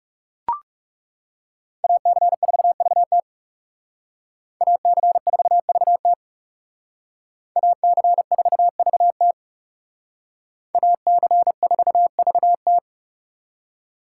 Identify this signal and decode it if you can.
Morse code, heard as a keyed tone, about 39 wpm, 700 Hz. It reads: AC4VT AC4VT AC4VT AC4VT